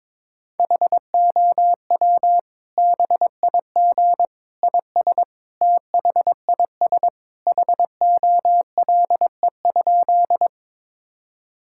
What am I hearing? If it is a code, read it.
Morse code, heard as a keyed tone, about 22 wpm, 700 Hz. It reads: HOW BIG IS THIS HOLE?
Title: Sentences from Top 1600 Words 22wpm